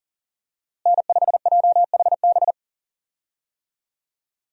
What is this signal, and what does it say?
Morse code, heard as a keyed tone, about 40 wpm, 700 Hz. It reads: N5JHB
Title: USA Call Signs - 5 Characters - Repeated 3x - 40wpm